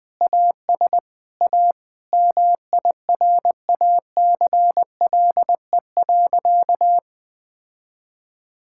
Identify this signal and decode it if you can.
Morse code, heard as a keyed tone, about 21 wpm, 700 Hz. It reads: AS A MIRACLE.
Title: Sentences with 7-Word Phrases - 20wpm